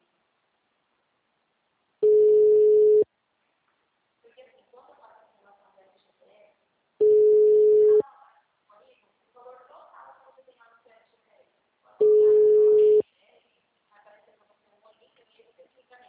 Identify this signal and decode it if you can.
Ringback tone (Europe, 425 Hz), with speech.